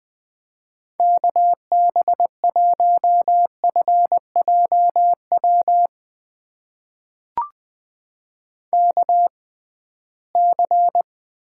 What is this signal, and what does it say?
Morse code, heard as a keyed tone, about 20 wpm, spaced as for 15 wpm, 700 Hz. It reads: KB1FJW KC
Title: USA Call Signs - 6 Characters - 6x Character Spacing - 20wpm